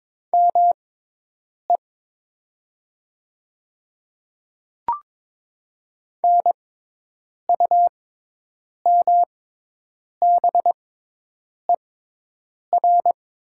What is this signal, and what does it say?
Morse code, heard as a keyed tone, about 22 wpm, spaced as for 8 wpm, 700 Hz. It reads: ME NUMBER
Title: 6x Character Spacing - Top 200 Words - 22wpm